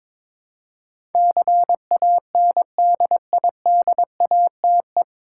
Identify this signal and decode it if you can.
Morse code, heard as a keyed tone, about 21 wpm, 700 Hz. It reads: CANDIDATE
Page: Morse Code Ninja practice 22wpm